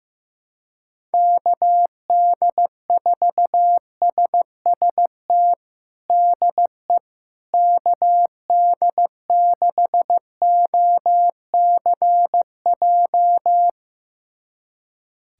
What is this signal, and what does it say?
Morse code, heard as a keyed tone, about 15 wpm, 700 Hz. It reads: KD4SST DE KD6OCJ
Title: USA Callsign DE Callsign - 6 Characters - 15wpm